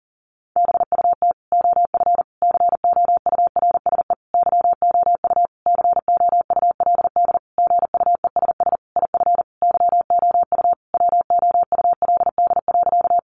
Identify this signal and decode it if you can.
Morse code, heard as a keyed tone, about 40 wpm, 700 Hz. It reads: BUT OF COURSE YOU COULD GUESS IF YOU WOULD.
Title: Sentences from Top 1500 Words 40wpm